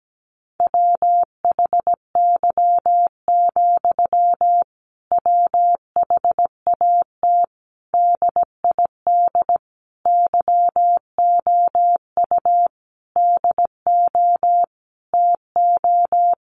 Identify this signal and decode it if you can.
Morse code, heard as a keyed tone, about 17 wpm, 700 Hz. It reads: WHY, WHAT DID YOU DO TO
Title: Sentences from Top 1500 Words 17wpm